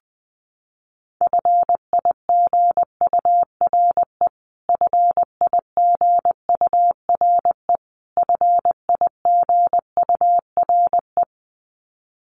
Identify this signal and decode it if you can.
Morse code, heard as a keyed tone, about 20 wpm, 700 Hz. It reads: FIGURE FIGURE FIGURE
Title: Top 500 Words Repeated 20wpm